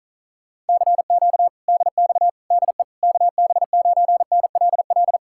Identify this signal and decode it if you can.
Morse code, heard as a keyed tone, about 46 wpm, 700 Hz. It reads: CQ DX DE KB9NLL